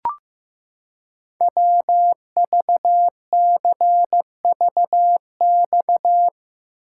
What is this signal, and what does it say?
Morse code, heard as a keyed tone, about 16 wpm, 700 Hz. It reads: WVCVX
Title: Code Groups - 5 Letter:Numbers - 15wpm